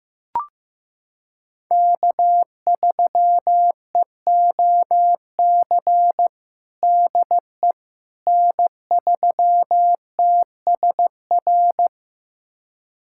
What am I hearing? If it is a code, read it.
Morse code, heard as a keyed tone, about 15 wpm, 700 Hz. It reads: K3EOC DE N3TSR
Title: USA Callsign DE Callsign - 5 Characters - 15wpm